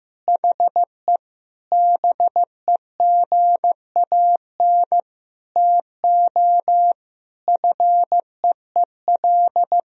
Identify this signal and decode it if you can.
Morse code, heard as a keyed tone, about 15 wpm, 700 Hz. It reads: HE BEGAN TO FEEL
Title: Sentences from Top 1000 Words - Review Part 1 - 15wpm